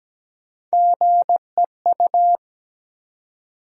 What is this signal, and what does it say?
Morse code, heard as a keyed tone, about 17 wpm, 700 Hz. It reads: GEU
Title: USA Call Sign Postfixes - 3 Letters 17wpm